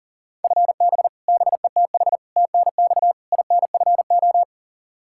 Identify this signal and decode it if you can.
Morse code, heard as a keyed tone, about 40 wpm, 700 Hz. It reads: FB BETH TNX INFO
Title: Practice QSO - Exchange 3 - More Info - 40wpm